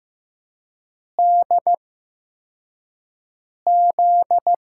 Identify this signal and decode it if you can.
Morse code, heard as a keyed tone, about 15 wpm, spaced as for 4 wpm, 700 Hz. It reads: DZ
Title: USA Call Signs - 6 Characters - 8x Character Spacing - 15wpm